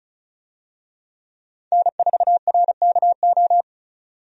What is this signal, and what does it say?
Morse code, heard as a keyed tone, about 36 wpm, 700 Hz. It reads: N4RKO
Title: USA Call Signs - 5 Characters - 35wpm